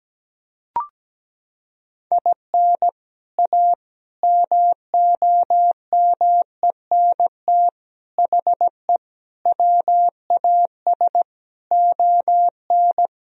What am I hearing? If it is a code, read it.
Morse code, heard as a keyed tone, about 17 wpm, 700 Hz. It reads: IN A MOMENT HE WAS ON
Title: Sentences from Top 1000 Words - Review Part 2 - 17wpm